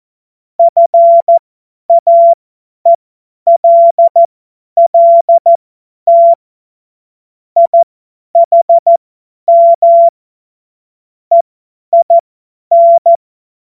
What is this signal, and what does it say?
Morse code, heard as a keyed tone, about 14 wpm, spaced as for 10 wpm, 680 Hz. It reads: FAELLT IHM EIN
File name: Morsekurs / Witze / Witz 18. Januar 2022, 50 F70